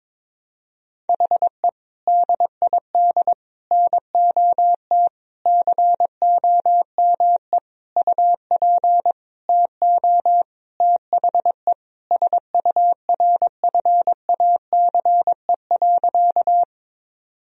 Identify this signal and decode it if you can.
Morse code, heard as a keyed tone, about 22 wpm, 700 Hz. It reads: HE DID NOT COME UP TO THE SURFACE.
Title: Sentences from Top 1000 Words - Review Part 3 - 22wpm